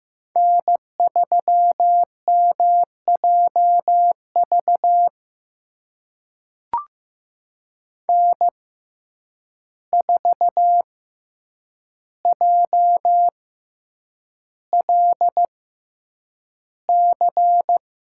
Morse code, heard as a keyed tone, about 15 wpm, spaced as for 8 wpm, 700 Hz. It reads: N3MJV N4JLC
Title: USA Call Signs - 5 Characters - 6x Character Spacing - 15wpm